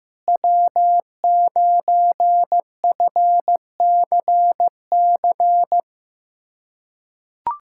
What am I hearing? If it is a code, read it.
Morse code, heard as a keyed tone, about 15 wpm, 700 Hz. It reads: W9FCC
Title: USA Callsign DE Callsign - 5 Characters - 15wpm